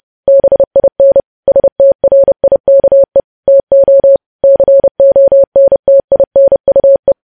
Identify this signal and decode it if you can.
Morse code, heard as a keyed tone, about 30 wpm, 550 Hz. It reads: BIN STRIKE TO CONTINUE
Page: news headlines in morse code at 30 wpm